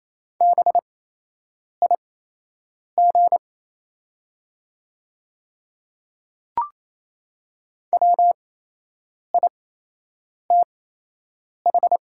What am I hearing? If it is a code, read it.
Morse code, heard as a keyed tone, about 28 wpm, spaced as for 8 wpm, 690 Hz. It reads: BIG WITH
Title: 8x Character Spacing - Top 200 Words - 28wpm